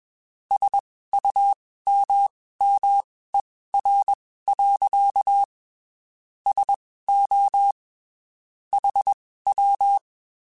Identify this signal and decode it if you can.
Morse code, heard as a keyed tone, about 21 wpm, spaced as for 15 wpm, 800 Hz. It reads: SUMMER. SO HW
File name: (spaced) / QSOs Spaced-mp3s / WD0FBT @20 sp